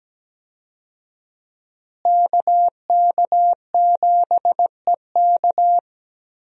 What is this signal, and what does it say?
Morse code, heard as a keyed tone, about 17 wpm, 700 Hz. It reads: KK7EK
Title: USA Call Signs - 5 Characters - Repeated 2x - 17wpm